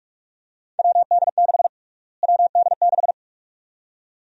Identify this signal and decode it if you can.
Morse code, heard as a keyed tone, about 46 wpm, 700 Hz. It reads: WD6 WD6
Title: USA Call Sign Prefixes - Repeated 2x - 45wpm